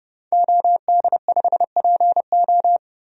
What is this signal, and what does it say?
Morse code, heard as a keyed tone, about 30 wpm, 700 Hz. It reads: OD5PO
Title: DXCC-100 Call Signs - 5 Characters - 30wpm